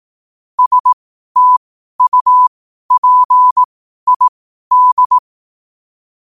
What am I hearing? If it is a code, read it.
Morse code, heard as a keyed tone, about 18 wpm, spaced as for 12 wpm, 1000 Hz. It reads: STUPID
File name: qotdfiles / 12wpm